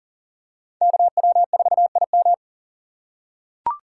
Morse code, heard as a keyed tone, about 39 wpm, 700 Hz. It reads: KW4IM
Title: USA Call Signs - 5 Characters - 4x Character Spacing - 40wpm